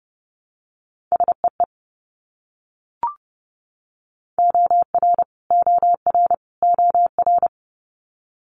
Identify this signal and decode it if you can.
Morse code, heard as a keyed tone, about 30 wpm, 690 Hz. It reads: SEE OR OR OR
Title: Multi-letter Phonemes Repeated 3x 30wpm